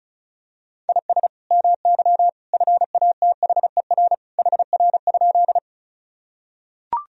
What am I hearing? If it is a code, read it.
Morse code, heard as a keyed tone, about 36 wpm, 700 Hz. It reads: IS MY FATHER HR?
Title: Sentences from Top 300 Words and Abbreviations 35wpm